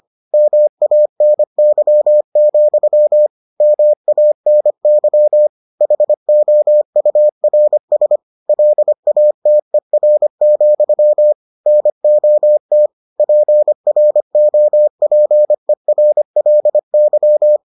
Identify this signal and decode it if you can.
Morse code, heard as a keyed tone, about 25 wpm, 600 Hz. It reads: MANY, MANY HOURS LATER, NOT PROPERLY